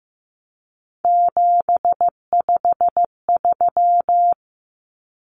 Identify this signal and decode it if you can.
Morse code, heard as a keyed tone, about 15 wpm, 700 Hz. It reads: 753
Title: Common Cut Numbers - 3-Digit 15wpm